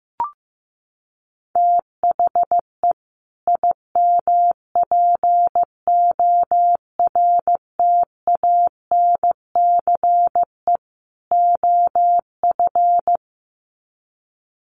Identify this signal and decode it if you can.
Morse code, heard as a keyed tone, about 15 wpm, 700 Hz. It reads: THE IMPORTANCE OF
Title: Sets of 3 Words 15wpm